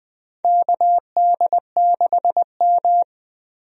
Morse code, heard as a keyed tone, about 20 wpm, 700 Hz. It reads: KD6M
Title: USA Call Signs - 4 Characters - 20wpm